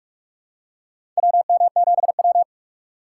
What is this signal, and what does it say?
Morse code, heard as a keyed tone, about 44 wpm, 700 Hz. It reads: WM7W
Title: USA Call Signs - 4 Characters - 2x Character Spacing - 45wpm